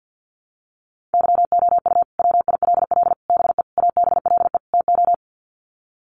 Morse code, heard as a keyed tone, about 50 wpm, 700 Hz. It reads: YOU WILL BE ABLE TO
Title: Sets of 5 Words Encore2 50wpm